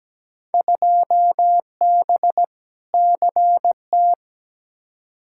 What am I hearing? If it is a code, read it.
Morse code, heard as a keyed tone, about 17 wpm, 700 Hz. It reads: 2B CT
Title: Field Day Exchanges Encore - 17wpm